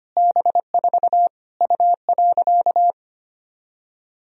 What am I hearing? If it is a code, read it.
Morse code, heard as a keyed tone, about 25 wpm, 700 Hz. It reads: B4 U.
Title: Sentences from Top 200 Words and Abbreviations 25wpm